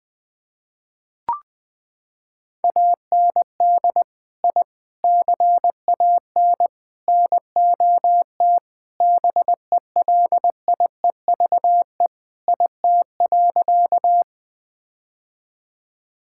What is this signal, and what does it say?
Morse code, heard as a keyed tone, about 20 wpm, 700 Hz. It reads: AND I CAN NOT BELIEVE IT.
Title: Sentences with 4-Word Phrases - 20wpm